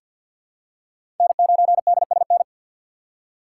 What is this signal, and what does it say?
Morse code, heard as a keyed tone, about 52 wpm, 700 Hz. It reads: N9DSN
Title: USA Call Signs - 5 Characters - 2x Character Spacing - 50wpm